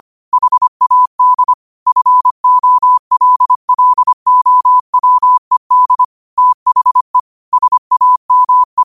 Morse code, heard as a keyed tone, about 25 wpm, 1000 Hz. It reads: HAD FOLLOWED THE SAME
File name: qotdfiles / 25wpm